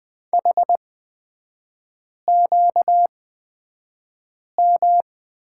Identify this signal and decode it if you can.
Morse code, heard as a keyed tone, about 20 wpm, spaced as for 5 wpm, 690 Hz. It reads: HQM
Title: 236 Lesson 38e - Letter q - Three Characters 20wpm